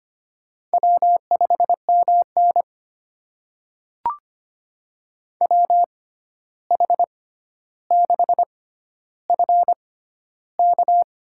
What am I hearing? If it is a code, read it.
Morse code, heard as a keyed tone, about 25 wpm, spaced as for 12 wpm, 700 Hz. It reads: W5MN WH6FK
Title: USA Call Signs - 6x Character Spacing - 25wpm